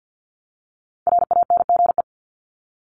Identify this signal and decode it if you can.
Morse code, heard as a keyed tone, about 48 wpm, 700 Hz. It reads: RANGE
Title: Top 501-750 Words 50wpm